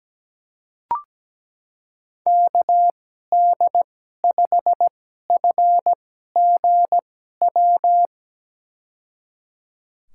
Morse code, heard as a keyed tone, about 17 wpm, spaced as for 12 wpm, 700 Hz. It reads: KD5FGW
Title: USA Call Signs - 6 Characters - 2x Character Spacing - 17wpm